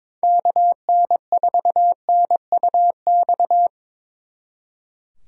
Morse code, heard as a keyed tone, about 25 wpm, 700 Hz. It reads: KN4NUX
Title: Speed Racing - USA 2x3 Call Signs 22wpm